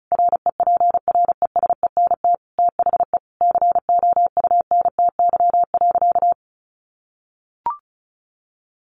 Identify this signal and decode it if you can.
Morse code, heard as a keyed tone, about 34 wpm, 700 Hz. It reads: REPRESENT THE COUNTY.
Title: Sentences from Top 1200 Words 35wpm